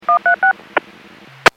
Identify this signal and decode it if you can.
DTMF, heard as telephone touch tones, tones of about 110 ms. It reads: 136
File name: WPS / 03157 13-01 WPS 451.25 WPS